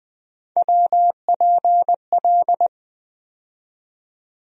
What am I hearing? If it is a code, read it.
Morse code, heard as a keyed tone, about 20 wpm, 700 Hz. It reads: WPL
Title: USA Call Sign Postfixes - 3 Letters 20wpm